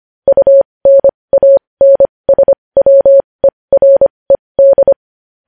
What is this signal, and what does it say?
Morse code, heard as a keyed tone, about 25 wpm, spaced as for 20 wpm, 550 Hz. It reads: UNANSWERED
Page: news headlines in morse code at 20 wpm